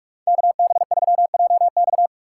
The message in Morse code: KB3JX

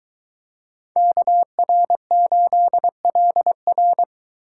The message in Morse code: KR8LR

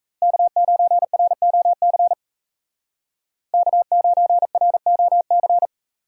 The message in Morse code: K9ROC K9ROC

